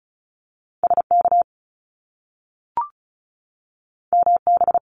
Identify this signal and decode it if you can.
Morse code, heard as a keyed tone, about 36 wpm, 700 Hz. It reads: SK MB
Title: Top Two Letter Combinations in Words 35wpm